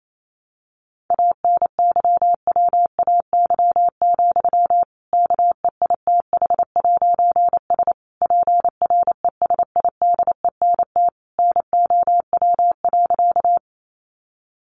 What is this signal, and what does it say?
Morse code, heard as a keyed tone, about 28 wpm, 700 Hz. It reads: ANYWAY, KEITH'S PRESIDENT NOW.
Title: A Sound Of Thunder 28wpm